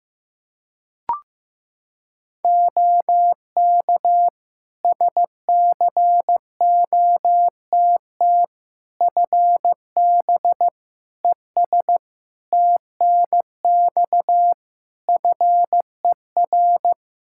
Morse code, heard as a keyed tone, about 15 wpm, 700 Hz. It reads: OK SCOTT FB ES TNX FER
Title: Practice QSO - Exchange 2 - Info - 15wpm